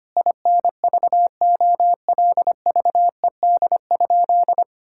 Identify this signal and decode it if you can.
Morse code, heard as a keyed tone, about 25 wpm, 700 Hz. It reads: INVOLVED?
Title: Sentences from Top 1000 Words - Review Part 5 - 25wpm